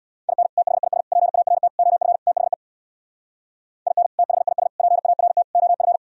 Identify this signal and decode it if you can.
Morse code, heard as a keyed tone, about 64 wpm, 700 Hz. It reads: IN THIS PART OF THE IN THIS PART OF